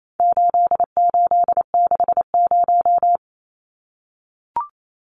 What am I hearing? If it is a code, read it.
Morse code, heard as a keyed tone, about 28 wpm, 700 Hz. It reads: 8860
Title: Numbers - 4-Digit 28wpm